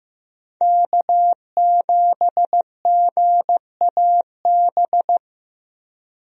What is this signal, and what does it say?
Morse code, heard as a keyed tone, about 15 wpm, 700 Hz. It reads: K7GAB
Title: USA Call Signs - 5 Characters - Repeated 2x - 15wpm